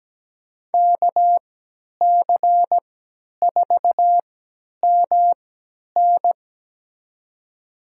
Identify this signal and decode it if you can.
Morse code, heard as a keyed tone, about 17 wpm, spaced as for 10 wpm, 700 Hz. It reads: KC4MN
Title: USA Call Signs - 5 Characters - 3x Character Spacing - 17wpm